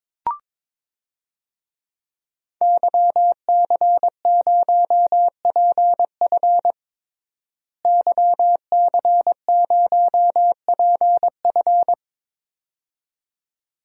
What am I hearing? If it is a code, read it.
Morse code, heard as a keyed tone, about 21 wpm, 700 Hz. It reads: YC0PF YC0PF
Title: DXCC-100 Call Signs - 5 Characters - Repeated 2x - 22wpm